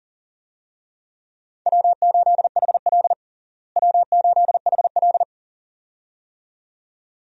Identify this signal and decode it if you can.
Morse code, heard as a keyed tone, about 40 wpm, 700 Hz. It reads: W8HL W8HL
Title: USA Call Signs - 4 Characters - Repeated 2x - 40wpm